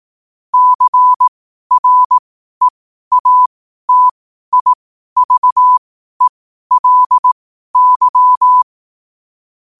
Morse code, heard as a keyed tone, about 18 wpm, spaced as for 12 wpm, 1000 Hz. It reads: CREATIVELY
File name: qotdfiles / 12wpm